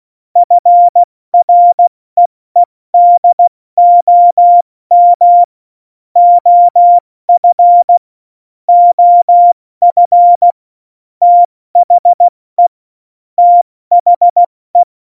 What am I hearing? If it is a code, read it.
Morse code, heard as a keyed tone, about 16 wpm, 700 Hz. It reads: FREEDOM OF OF THE THE
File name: Morsekurs / Witze / Witz 25 Juli 17, Tempo 70 F80